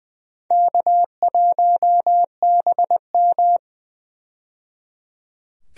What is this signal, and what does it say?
Morse code, heard as a keyed tone, about 20 wpm, 700 Hz. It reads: K1BM
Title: USA Call Signs - 4 Characters - 20wpm